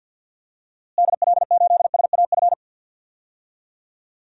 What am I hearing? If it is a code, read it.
Morse code, heard as a keyed tone, about 52 wpm, 700 Hz. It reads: DL8SAF
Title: DXCC-100 Call Signs - 6 Characters - Repeated 3x - 50wpm